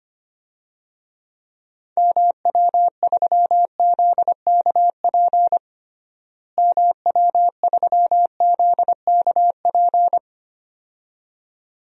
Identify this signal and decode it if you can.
Morse code, heard as a keyed tone, about 25 wpm, 700 Hz. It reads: MW3ZKP MW3ZKP
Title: DXCC-100 Call Signs - 6 Characters - Repeated 2x - 25wpm